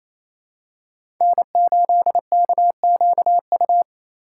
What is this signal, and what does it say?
Morse code, heard as a keyed tone, about 28 wpm, 700 Hz. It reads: N8KQU
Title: USA Call Signs - 5 Characters - 2x Character Spacing - 28wpm